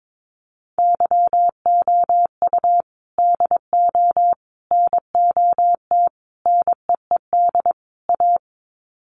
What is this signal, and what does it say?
Morse code, heard as a keyed tone, about 22 wpm, 700 Hz. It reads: YOU DO NOT NEED A
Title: Sets of 5 Words Encore 22wpm